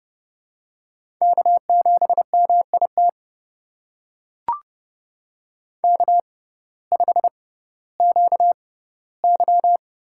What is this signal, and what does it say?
Morse code, heard as a keyed tone, about 30 wpm, spaced as for 17 wpm, 700 Hz. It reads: K7MIT K5QY
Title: USA Call Signs - 6x Character Spacing - 30wpm